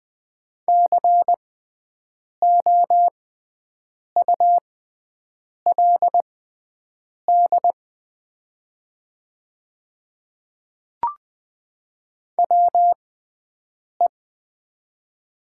Morse code, heard as a keyed tone, about 20 wpm, spaced as for 7 wpm, 690 Hz. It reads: COULD WE
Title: 6x Character Spacing - Top 100 Words - 20wpm